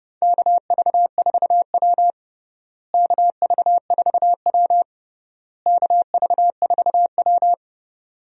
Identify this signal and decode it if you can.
Morse code, heard as a keyed tone, about 31 wpm, 700 Hz. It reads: KV4W KV4W KV4W
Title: USA Call Signs - 4 Characters - Repeated 3x - 30wpm